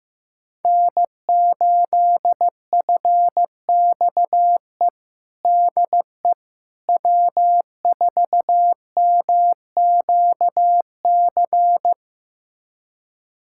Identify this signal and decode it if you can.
Morse code, heard as a keyed tone, about 15 wpm, 700 Hz. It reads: N8FXE DE W4MQC
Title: USA Callsign DE Callsign 15wpm